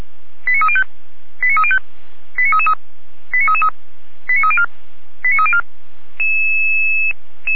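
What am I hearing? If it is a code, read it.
Five-tone selective call (ZVEI), heard as quick runs of five tones, tones of about 70 ms. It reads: 89305 89305 89333 89304 89304
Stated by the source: ZVEI five-tone call 89303 received on SDR.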